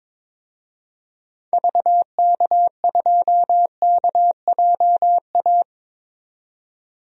Morse code, heard as a keyed tone, about 22 wpm, 700 Hz. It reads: VK2KJA
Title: DXCC-100 Call Signs - 6 Characters - 22wpm